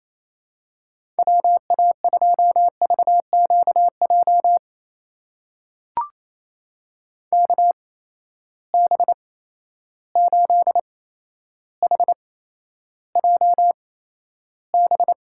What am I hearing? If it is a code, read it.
Morse code, heard as a keyed tone, about 28 wpm, spaced as for 12 wpm, 700 Hz. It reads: WA2VQJ KB8HJB
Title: USA Call Signs - 6 Characters - 8x Character Spacing - 28wpm